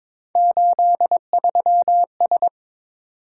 Morse code, heard as a keyed tone, about 27 wpm, 700 Hz. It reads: 83S
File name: Speed-Racing - Three Letter-Number Combinations - 22wpm